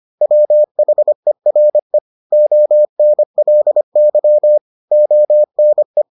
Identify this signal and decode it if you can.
Morse code, heard as a keyed tone, about 25 wpm, 600 Hz. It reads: WHERE ONLY ONE